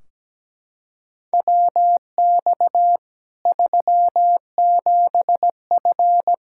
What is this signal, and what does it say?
Morse code, heard as a keyed tone, about 17 wpm, 700 Hz. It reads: WX 37F
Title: Practice QSO - Exchange 2 - Info - 17wpm